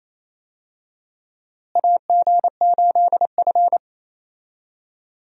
Morse code, heard as a keyed tone, about 28 wpm, 700 Hz. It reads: AG8F
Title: USA Call Signs - 4 Characters - Repeated 3x - 28wpm